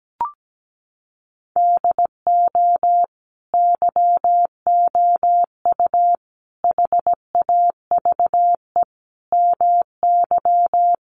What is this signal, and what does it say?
Morse code, heard as a keyed tone, about 17 wpm, 700 Hz. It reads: DO YOU HAVE MY
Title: Sentences from Top 1400 Words 17wpm